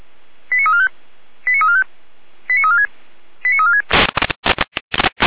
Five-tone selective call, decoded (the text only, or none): 89345 89345 89346 89346